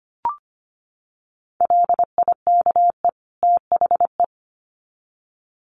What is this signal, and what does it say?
Morse code, heard as a keyed tone, about 25 wpm, 700 Hz. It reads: LIKE THE
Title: Sets of 2 Words 25wpm